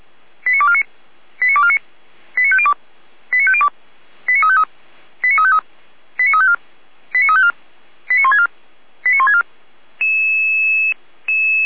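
Five-tone selective call (ZVEI), heard as quick runs of five tones, tones of about 70 ms. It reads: 89249 89249 89502 89502 89353 89353 89354 89354 89164 89164
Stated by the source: ZVEI five-tone code 89250 received on an SDR, fire brigade Orlamünde (feuerwehr-orlamuende).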